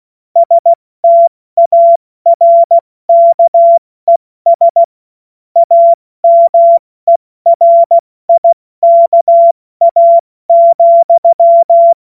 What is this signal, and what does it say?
Morse code, heard as a keyed tone, about 16 wpm, 680 Hz. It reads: STARKES AMERIKA,
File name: Morsekurs / Witze / Witz 09. Februar 2021, 70 F80